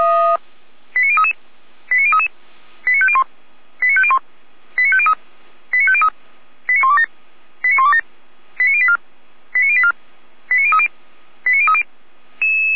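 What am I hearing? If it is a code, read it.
Five-tone selective call (ZVEI), heard as quick runs of five tones, tones of about 80 ms. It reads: 89033 89033 89501 89501 89503 89503 89127 89127 89084 89084 89030 89030
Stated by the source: ZVEI five-tone selective call received on SDR.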